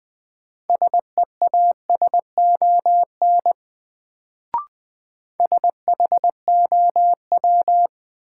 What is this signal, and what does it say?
Morse code, heard as a keyed tone, about 20 wpm, 700 Hz. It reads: SEASON SHOW
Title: Sets of 5 Words - New Word Drill - 20wpm